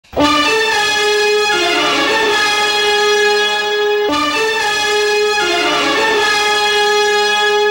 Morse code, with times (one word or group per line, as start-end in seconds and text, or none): none